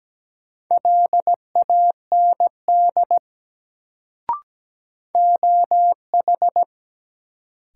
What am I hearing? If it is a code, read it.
Morse code, heard as a keyed tone, about 17 wpm, 700 Hz. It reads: LAND OH
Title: Sentences from Top 500 Words - New Word Drill - 17wpm